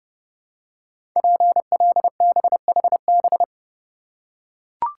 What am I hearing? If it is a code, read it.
Morse code, heard as a keyed tone, about 30 wpm, 700 Hz. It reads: PLBHB
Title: Code Groups - 5 Letter:Numbers - 30wpm